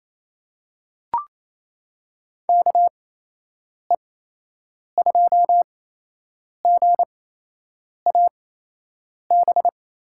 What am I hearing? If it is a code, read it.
Morse code, heard as a keyed tone, about 27 wpm, spaced as for 8 wpm, 700 Hz. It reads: KE2GAB
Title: USA Call Signs - 6 Characters - 8x Character Spacing - 28wpm